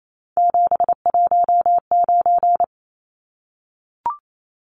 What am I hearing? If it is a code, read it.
Morse code, heard as a keyed tone, about 28 wpm, 700 Hz. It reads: 719
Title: Numbers - 3-Digit 28wpm